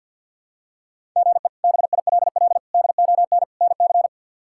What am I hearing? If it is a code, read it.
Morse code, heard as a keyed tone, about 49 wpm, 700 Hz. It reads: GE BILL DON NC
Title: K1USN SST QSO Practice - 50wpm